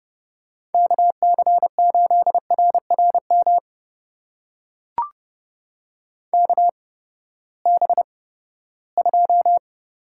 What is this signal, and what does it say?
Morse code, heard as a keyed tone, about 30 wpm, spaced as for 17 wpm, 700 Hz. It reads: KC8RRM KB2
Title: USA Call Signs - 6 Characters - 8x Character Spacing - 30wpm